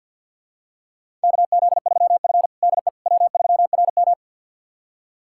Morse code, heard as a keyed tone, about 52 wpm, 700 Hz. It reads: KZ3F DE W3RM
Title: USA Callsign DE Callsign - 4 Characters - 50wpm